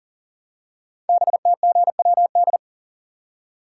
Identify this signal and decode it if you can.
Morse code, heard as a keyed tone, about 38 wpm, 690 Hz. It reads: BTGWD
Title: Code Groups - 5 Letters - 40wpm